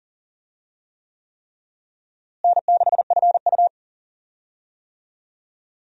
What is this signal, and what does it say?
Morse code, heard as a keyed tone, about 40 wpm, 700 Hz. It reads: N6FU
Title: USA Call Signs - 4 Characters - Repeated 3x - 40wpm